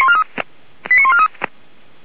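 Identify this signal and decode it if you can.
Five-tone selective call (ZVEI), heard as quick runs of five tones, tones of about 70 ms. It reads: 89143
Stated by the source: ZVEI five-tone selective call received with an SDR.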